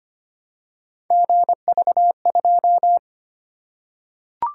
Morse code, heard as a keyed tone, about 25 wpm, 700 Hz. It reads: GV2
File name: Three Letter-Number Combinations - 25wpm